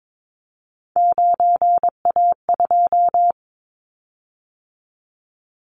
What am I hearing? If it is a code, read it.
Morse code, heard as a keyed tone, about 22 wpm, 700 Hz. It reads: 9A2
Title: Callsign Prefixes for Top 50 DXCC - 22wpm